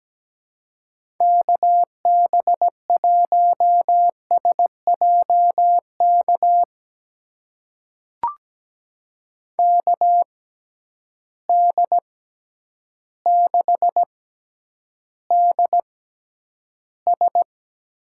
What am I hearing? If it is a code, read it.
Morse code, heard as a keyed tone, about 17 wpm, spaced as for 9 wpm, 700 Hz. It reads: KB1SJK KD6DS
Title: USA Call Signs - 6 Characters - 6x Character Spacing - 17wpm